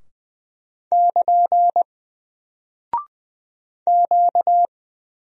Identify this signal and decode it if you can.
Morse code, heard as a keyed tone, about 20 wpm, 700 Hz. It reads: <KN> Q
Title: Single Character (Letter:Number:Punctuation:Prosign) 20wpm